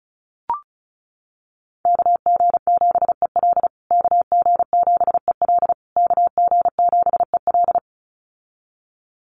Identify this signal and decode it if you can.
Morse code, heard as a keyed tone, about 34 wpm, 700 Hz. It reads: KG7EL KG7EL KG7EL
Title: USA Call Signs Repeated 3x 35wpm